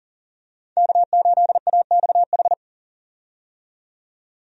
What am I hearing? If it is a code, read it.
Morse code, heard as a keyed tone, about 40 wpm, 700 Hz. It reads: K8AXH
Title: USA Call Signs - 5 Characters - 40wpm